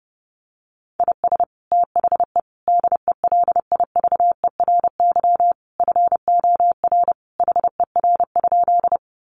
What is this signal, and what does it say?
Morse code, heard as a keyed tone, about 30 wpm, 700 Hz. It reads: IS THE DELIVERY FOR HER?
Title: Sentences from Top 1500 Words 30wpm